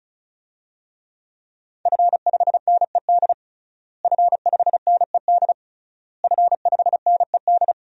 Morse code, heard as a keyed tone, about 34 wpm, 700 Hz. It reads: F5NED F5NED F5NED
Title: DXCC-100 Call Signs - 5 Characters - Repeated 3x - 35wpm